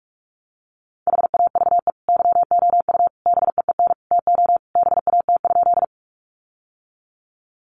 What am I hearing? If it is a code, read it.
Morse code, heard as a keyed tone, about 44 wpm, 700 Hz. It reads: HAVE YOU BEEN TO BAT?
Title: Sentences from Top 1500 Words 45wpm